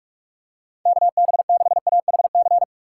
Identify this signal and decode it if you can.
Morse code, heard as a keyed tone, about 46 wpm, 700 Hz. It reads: KB6AHC
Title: USA Call Signs - 6 Characters - 3x Character Spacing - 45wpm